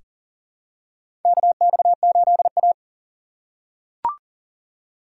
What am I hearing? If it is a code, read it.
Morse code, heard as a keyed tone, about 40 wpm, 700 Hz. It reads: KX8A
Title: USA Call Signs - 6x Character Spacing - 40wpm